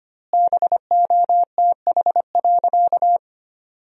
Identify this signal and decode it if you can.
Morse code, heard as a keyed tone, about 25 wpm, 700 Hz. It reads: BOTH.
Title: 2x Word-Spacing - Sentences from Top 300 Words - 25wpm